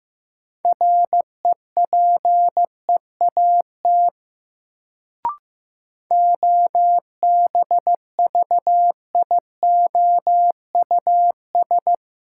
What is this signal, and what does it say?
Morse code, heard as a keyed tone, about 15 wpm, 700 Hz. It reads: REPEAT OBVIOUS
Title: Sentences from Top 1600 Words - New Word Drill - 15wpm